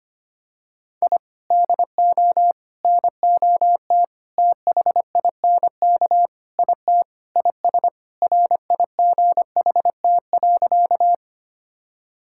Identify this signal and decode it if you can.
Morse code, heard as a keyed tone, about 25 wpm, 700 Hz. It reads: I DO NOT THINK IT IS RIGHT.
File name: practice / 25wpm / Sentences with 4-Word Phrases - 25wpm